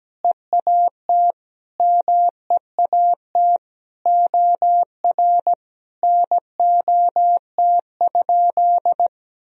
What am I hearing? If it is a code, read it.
Morse code, heard as a keyed tone, about 17 wpm, 700 Hz. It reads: EAT MEAT OR NOT?
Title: Sentences from Top 1000 Words - Review Part 3 - 17wpm